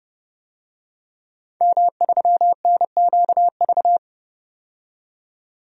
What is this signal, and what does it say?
Morse code, heard as a keyed tone, about 30 wpm, 700 Hz. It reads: M3NQV
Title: DXCC-100 Call Signs - 5 Characters - 30wpm